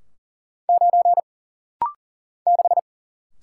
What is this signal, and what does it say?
Morse code, heard as a keyed tone, about 39 wpm, 700 Hz. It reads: 9 6
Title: Single Letter-Number - Rapid-Fire 40wpm